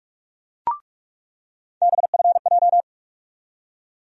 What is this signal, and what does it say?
Morse code, heard as a keyed tone, about 46 wpm, 700 Hz. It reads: BFJ
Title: USA Call Sign Postfixes - 3 Letters 45wpm